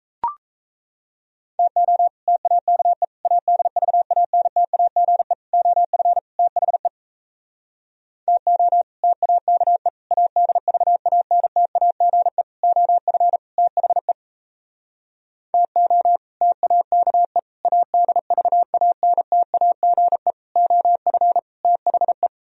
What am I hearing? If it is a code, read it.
Morse code, heard as a keyed tone, about 37 wpm, 700 Hz. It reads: TO TAKE ADVANTAGE OF THE TO TAKE ADVANTAGE OF THE TO TAKE ADVANTAGE OF THE